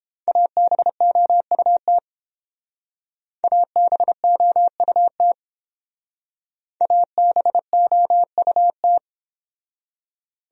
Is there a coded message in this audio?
Yes: Morse code from 0.3 to 9.0 s, heard as a keyed tone, about 30 wpm, 700 Hz. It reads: ABOUT ABOUT ABOUT